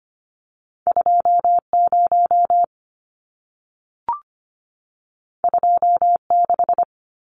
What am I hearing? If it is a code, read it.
Morse code, heard as a keyed tone, about 25 wpm, 700 Hz. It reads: 20 26
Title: Numbers - 2-Digit 25wpm